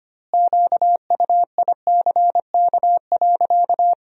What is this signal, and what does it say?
Morse code, heard as a keyed tone, about 25 wpm, 700 Hz. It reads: QUICK.
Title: 2x Word-Spacing - Sentences from Top 500 Words - 25wpm